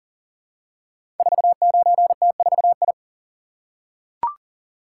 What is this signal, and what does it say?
Morse code, heard as a keyed tone, about 39 wpm, 700 Hz. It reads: 49T4I
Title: Code Groups - 5 Letter:Numbers - 40wpm